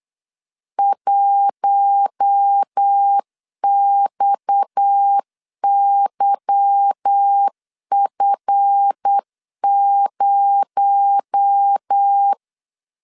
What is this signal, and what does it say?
Morse code, heard as a keyed tone, about 8 wpm, 800 Hz. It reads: 1XYF0
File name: LEZIONI / 8 YJ45 / 8 YJ 8wpm 3313